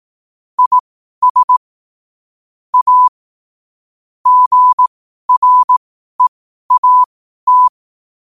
Morse code, heard as a keyed tone, about 18 wpm, spaced as for 12 wpm, 1000 Hz. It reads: IS A GREAT
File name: qotdfiles / 12wpm